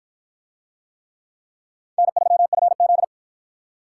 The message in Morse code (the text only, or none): N3FZ